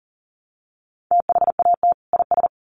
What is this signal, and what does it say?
Morse code, heard as a keyed tone, about 40 wpm, 700 Hz. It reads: THAT IS
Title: Sets of 2 Words 40wpm